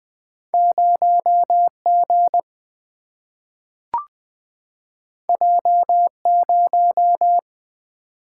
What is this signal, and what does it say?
Morse code, heard as a keyed tone, about 20 wpm, 700 Hz. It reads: 0 G J 0
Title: Single-Letter Single-Number 20wpm